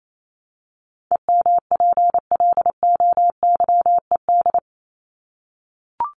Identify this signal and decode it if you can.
Morse code, heard as a keyed tone, about 27 wpm, 700 Hz. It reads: EMPLOYED
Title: ED Suffix - Top 100 Words with Base Words 28wpm